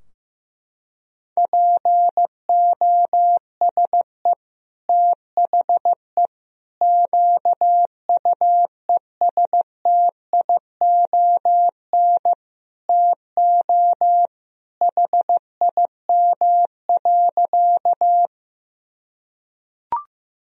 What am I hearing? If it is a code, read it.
Morse code, heard as a keyed tone, about 16 wpm, 700 Hz. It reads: POSE THE QUESTION TO HIM.
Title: Sentences from Top 1000 Words - Review Part 2 - 15wpm